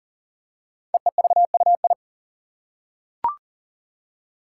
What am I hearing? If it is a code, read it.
Morse code, heard as a keyed tone, about 40 wpm, 700 Hz. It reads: EEVUI
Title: Code Groups - 5 Letters - 40wpm